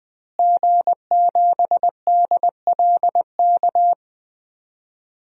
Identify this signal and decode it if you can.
Morse code, heard as a keyed tone, about 21 wpm, 700 Hz. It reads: G7DLK
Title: DXCC-100 Call Signs - 5 Characters - Repeated 2x - 20wpm